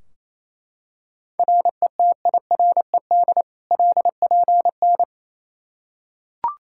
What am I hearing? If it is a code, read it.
Morse code, heard as a keyed tone, about 27 wpm, 700 Hz. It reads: RETIRED LPN
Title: Practice QSO - Exchange 3 - More Info - 28wpm